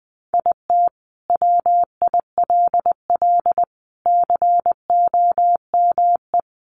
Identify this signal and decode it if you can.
Morse code, heard as a keyed tone, about 20 wpm, 700 Hz. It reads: IT WILL COME
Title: Sentences from Top 1400 Words 20wpm